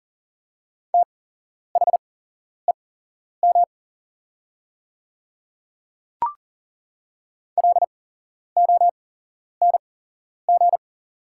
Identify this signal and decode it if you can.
Morse code, heard as a keyed tone, about 40 wpm, spaced as for 11 wpm, 700 Hz. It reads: THEM LONG
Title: 8x Character Spacing - Top 100 Words - 40wpm